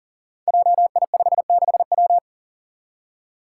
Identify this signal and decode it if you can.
Morse code, heard as a keyed tone, about 40 wpm, 700 Hz. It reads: JI56W
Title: Five Challenging Letter-Number Combinations 40wpm